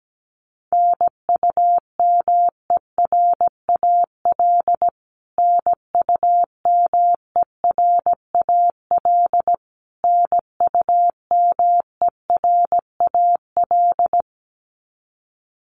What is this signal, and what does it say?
Morse code, heard as a keyed tone, about 17 wpm, 700 Hz. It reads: NUMERAL NUMERAL NUMERAL
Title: Top 500 Words Repeated 17wpm